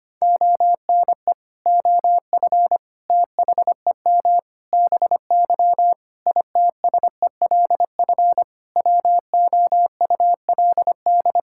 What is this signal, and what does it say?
Morse code, heard as a keyed tone, about 25 wpm, 700 Hz. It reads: ONE OF THEM BY ITSELF WOULD
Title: Sentences from Top 1000 Words - Review Part 5 - 25wpm